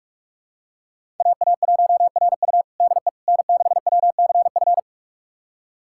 Morse code, heard as a keyed tone, about 46 wpm, 700 Hz. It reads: AA1RU DE N6WCF
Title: USA Callsign DE Callsign - 5 Characters - 45wpm